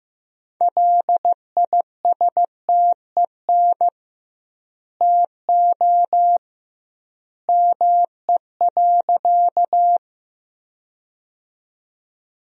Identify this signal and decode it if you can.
Morse code, heard as a keyed tone, about 15 wpm, 700 Hz. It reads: LISTEN TO ME.
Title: 2x Word-Spacing - Sentences from Top 400 Words - 15wpm